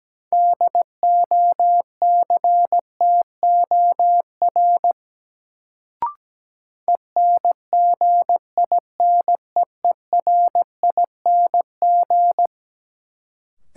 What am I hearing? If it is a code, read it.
Morse code, heard as a keyed tone, about 17 wpm, 700 Hz. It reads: DOCTOR ENGINEERING
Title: Sentences from Top 1500 Words - New Word Drill - 17wpm